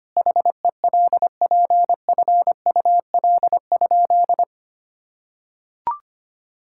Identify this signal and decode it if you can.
Morse code, heard as a keyed tone, about 25 wpm, 700 Hz. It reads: HELPFUL?
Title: Sentences from Top 1000 Words - Review Part 6 - 25wpm